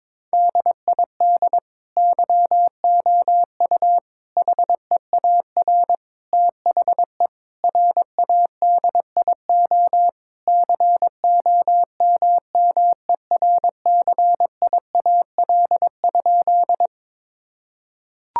Morse code, heard as a keyed tone, about 22 wpm, 700 Hz. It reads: DID YOU HEAR THE RADIO COMMERCIAL?
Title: Sentences from Top 1000 Words - Review Part 5 - 22wpm